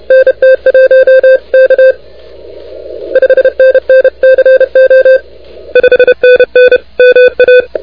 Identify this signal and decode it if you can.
Morse code, heard as a keyed tone, about 31 wpm, 530 Hz. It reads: NT1K 5NN CO 5NN MA
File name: recordings / K0FX de NT1K ARRL10M CW 28MHz